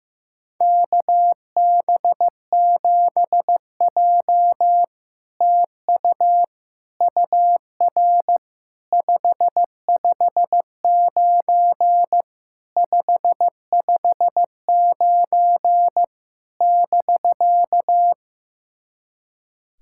Morse code, heard as a keyed tone, about 15 wpm, 690 Hz. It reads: KB7J TU UR 559 559 <BK>
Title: SOTA QSO Practice 15wpm